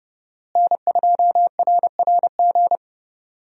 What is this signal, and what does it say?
Morse code, heard as a keyed tone, about 31 wpm, 700 Hz. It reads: N2RRG
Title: USA Call Signs - 5 Characters - Repeated 3x - 30wpm